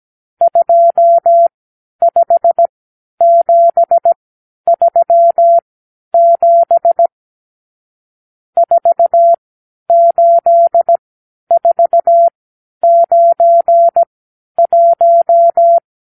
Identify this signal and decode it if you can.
Morse code, heard as a keyed tone, about 17 wpm, spaced as for 10 wpm, 680 Hz. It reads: 25737 48491
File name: arrl / Review (1-0)